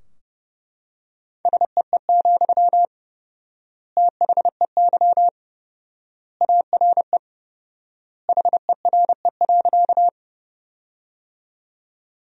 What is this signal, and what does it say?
Morse code, heard as a keyed tone, about 32 wpm, 700 Hz. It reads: SEE, THEY ARE HERE.
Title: 4x Word-Spacing - Sentences from Top 200 Words - 30wpm